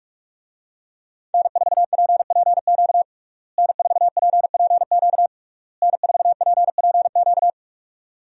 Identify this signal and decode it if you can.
Morse code, heard as a keyed tone, about 47 wpm, 700 Hz. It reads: N4PPQ N4PPQ N4PPQ
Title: USA Call Signs - 5 Characters - Repeated 3x - 45wpm